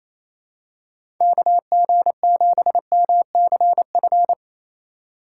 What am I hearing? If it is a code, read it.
Morse code, heard as a keyed tone, about 28 wpm, 700 Hz. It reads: KG7MCF
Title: USA Call Signs - 6 Characters - 3x Character Spacing - 28wpm